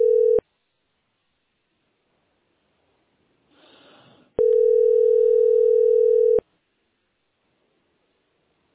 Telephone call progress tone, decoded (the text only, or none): ringback tone